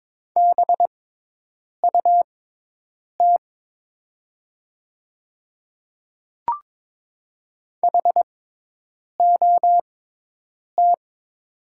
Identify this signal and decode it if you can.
Morse code, heard as a keyed tone, about 22 wpm, spaced as for 8 wpm, 700 Hz. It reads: BUT HOT
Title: 6x Character Spacing - Top 200 Words - 22wpm